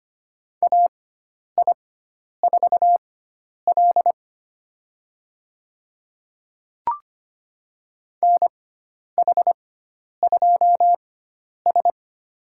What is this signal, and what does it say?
Morse code, heard as a keyed tone, about 25 wpm, spaced as for 10 wpm, 700 Hz. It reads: AI4L NH2S
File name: Lesson 16g - Letter c - Call Signs 25wpm